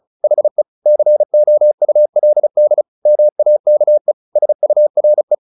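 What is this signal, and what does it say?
Morse code, heard as a keyed tone, about 34 wpm, 600 Hz. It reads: HE COULD MAKE SURE
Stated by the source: Morse practice at 35 wpm from CW med Gnister, sending Ken Follett's novel Never in English.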